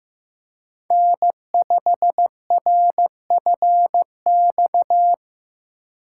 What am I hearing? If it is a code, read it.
Morse code, heard as a keyed tone, about 15 wpm, 700 Hz. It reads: N5RFX
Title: USA Call Signs - 5 Characters - 2x Character Spacing - 15wpm